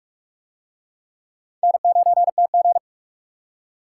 Morse code, heard as a keyed tone, about 44 wpm, 700 Hz. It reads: N9TG